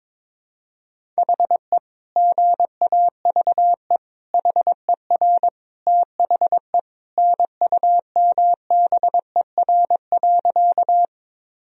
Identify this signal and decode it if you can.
Morse code, heard as a keyed tone, about 22 wpm, 700 Hz. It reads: HE GAVE HER THE NUMBER.
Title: Sentences from Top 1000 Words - Review Part 1 - 22wpm